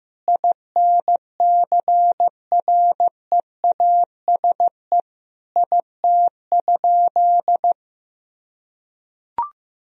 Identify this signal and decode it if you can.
Morse code, heard as a keyed tone, about 15 wpm, 700 Hz. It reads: INCREASE IT?
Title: Sentences from Top 1000 Words - Review Part 4 - 15wpm